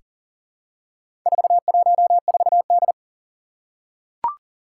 Morse code, heard as a keyed tone, about 40 wpm, 700 Hz. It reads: 414D
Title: Four Challenging Letter-Number Combinations 40wpm